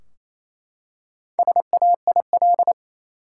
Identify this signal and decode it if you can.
Morse code, heard as a keyed tone, about 28 wpm, 700 Hz. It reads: SAIL
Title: Four Letter Words - 28wpm